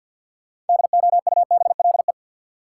Morse code, heard as a keyed tone, about 52 wpm, 700 Hz. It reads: DOUBLE
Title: Sentences from Top 1400 Words - New Word Drill - 50wpm